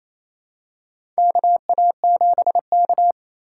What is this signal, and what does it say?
Morse code, heard as a keyed tone, about 28 wpm, 700 Hz. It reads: KA7K
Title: USA Call Signs - 4 Characters - 2x Character Spacing - 28wpm